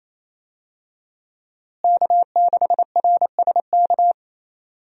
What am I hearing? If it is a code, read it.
Morse code, heard as a keyed tone, about 28 wpm, 700 Hz. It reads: K6RSK
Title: USA Call Signs - 5 Characters - Repeated 2x - 28wpm